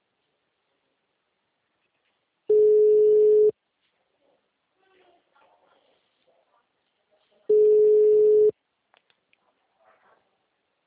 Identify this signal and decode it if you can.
Ringback tone (Europe, 425 Hz).
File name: out- -2006- - - .98